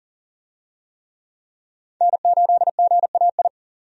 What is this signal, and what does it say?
Morse code, heard as a keyed tone, about 40 wpm, 700 Hz. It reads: N8GAI
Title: USA Call Signs - 5 Characters - 40wpm